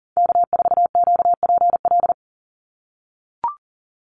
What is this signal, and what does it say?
Morse code, heard as a keyed tone, about 40 wpm, 700 Hz. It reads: K4QPL
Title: Call Signs - 40wpm